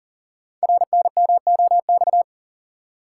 Morse code, heard as a keyed tone, about 42 wpm, 690 Hz. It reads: RNMOX
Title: Code Groups - 5 Letters - 40wpm